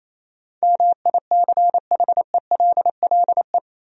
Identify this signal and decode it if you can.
Morse code, heard as a keyed tone, about 28 wpm, 700 Hz. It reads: MICHELLE
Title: USA Common Names - 28wpm